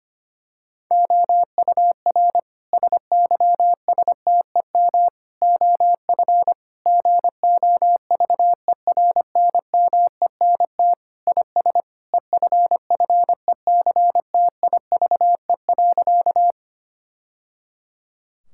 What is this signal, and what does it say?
Morse code, heard as a keyed tone, about 25 wpm, 700 Hz. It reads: OUR SYSTEM OF GOVERNMENT IS EFFECTIVE.
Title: Sentences from Top 1000 Words - Review Part 6 - 25wpm